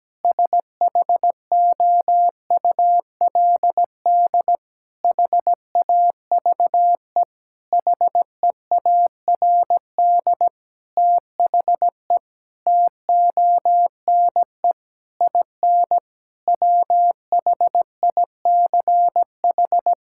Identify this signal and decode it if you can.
Morse code, heard as a keyed tone, about 17 wpm, 700 Hz. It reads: SHOULD HAVE HEARD THE TONE IN WHICH
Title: Sentences from Top 1600 Words 17wpm